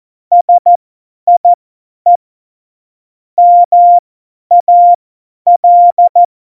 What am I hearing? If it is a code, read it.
Morse code, heard as a keyed tone, about 14 wpm, spaced as for 10 wpm, 700 Hz. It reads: SIE MAL
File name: Morsekurs / Witze / Witz 08. September 2020, 50 F70